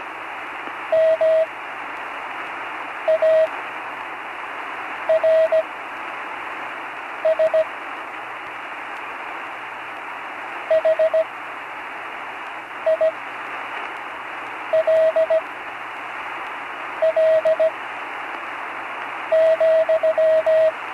Morse code, heard as a keyed tone, about 17 wpm, spaced as for 5 wpm, 640 Hz, with radio noise. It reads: MARS HILL,
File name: CW Learning Programs / Learning Morse Code (from Tune in the World) / 35 answers